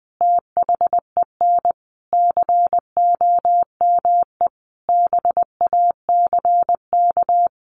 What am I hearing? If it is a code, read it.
Morse code, heard as a keyed tone, about 20 wpm, 700 Hz. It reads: THEN COME BACK
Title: Sentences from Top 600 Words 20wpm